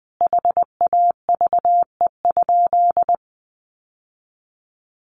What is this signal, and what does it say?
Morse code, heard as a keyed tone, about 20 wpm, 700 Hz. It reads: HAVE?
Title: Sentences from Top 1000 Words 20wpm